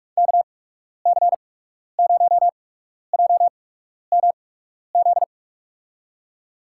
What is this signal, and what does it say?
Morse code, heard as a keyed tone, about 42 wpm, spaced as for 12 wpm, 700 Hz. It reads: KC0JMZ
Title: USA Call Signs - 6 Characters - 8x Character Spacing - 45wpm